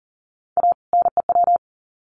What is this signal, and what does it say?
Morse code, heard as a keyed tone, about 40 wpm, 700 Hz. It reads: A NEW
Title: Sets of 2 Words 40wpm